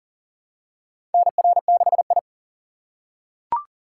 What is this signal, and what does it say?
Morse code, heard as a keyed tone, about 40 wpm, 700 Hz. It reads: NR6I
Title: USA Call Signs - 4 Characters - 3x Character Spacing - 40wpm